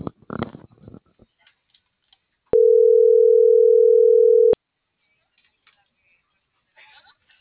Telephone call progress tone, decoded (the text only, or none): ringback tone